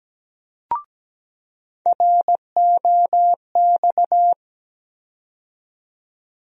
Morse code, heard as a keyed tone, about 17 wpm, 700 Hz. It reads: ROX
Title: USA Call Sign Postfixes - 3 Letters 17wpm